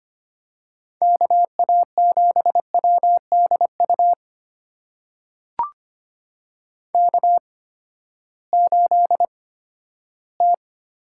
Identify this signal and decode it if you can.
Morse code, heard as a keyed tone, about 25 wpm, spaced as for 14 wpm, 700 Hz. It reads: KA7WDU K8T
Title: USA Call Signs - 8x Character Spacing - 25wpm